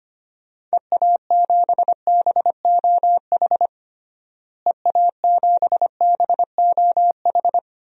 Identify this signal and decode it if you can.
Morse code, heard as a keyed tone, about 25 wpm, 700 Hz. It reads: EA7BOH EA7BOH
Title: DXCC-100 Call Signs - 6 Characters - Repeated 3x - 25wpm